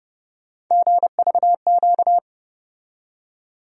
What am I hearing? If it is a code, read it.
Morse code, heard as a keyed tone, about 32 wpm, 700 Hz. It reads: GVQ
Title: USA Call Sign Postfixes - 3 Letters 30wpm